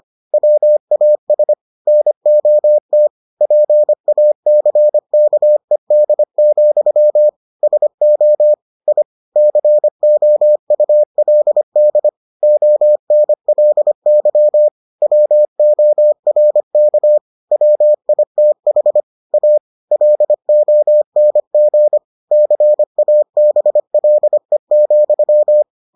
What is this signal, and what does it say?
Morse code, heard as a keyed tone, about 25 wpm, 600 Hz. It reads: WAS NOT PACKED, SO I COULD ONLY WORK WITH A LONG CABLE,